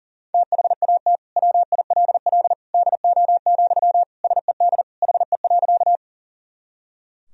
Morse code, heard as a keyed tone, about 37 wpm, 700 Hz. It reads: THAT WILL DO, SED HE.
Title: Sentences from Top 100 Words and Abbreviations 40wpm